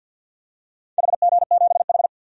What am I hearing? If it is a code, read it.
Morse code, heard as a keyed tone, about 47 wpm, 700 Hz. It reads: HG7H
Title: DXCC-100 Call Signs - 4 Characters - 50wpm